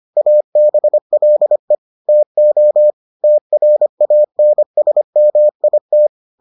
Morse code, heard as a keyed tone, about 25 wpm, 600 Hz. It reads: ABLE TO TRANSMIT